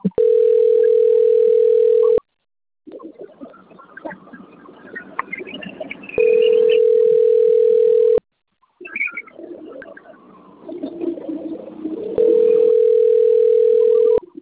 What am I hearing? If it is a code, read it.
Ringback tone (North America), 440 + 480 Hz.